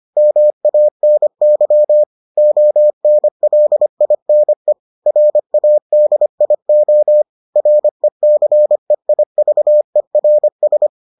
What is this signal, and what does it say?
Morse code, heard as a keyed tone, about 25 wpm, 600 Hz. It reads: MANY ONLINE RADIO RECEIVERS